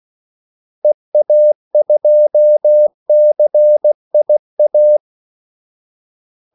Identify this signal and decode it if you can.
Morse code, heard as a keyed tone, about 16 wpm, 600 Hz. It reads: EA2CIA
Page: Kaldesignaler 16 wpm